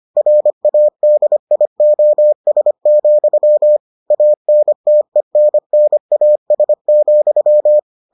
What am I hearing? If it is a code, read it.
Morse code, heard as a keyed tone, about 25 wpm, 600 Hz. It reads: RADIOS, ANTENNAS,